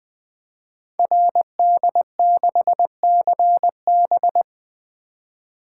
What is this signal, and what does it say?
Morse code, heard as a keyed tone, about 20 wpm, 700 Hz. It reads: RD6CB
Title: Code Groups - 5 Letter:Numbers - 20wpm